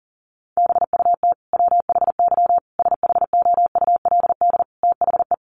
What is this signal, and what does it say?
Morse code, heard as a keyed tone, about 40 wpm, 700 Hz. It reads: BUT WHY SHOULD THE
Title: Sentences from Top 1100 Words 40wpm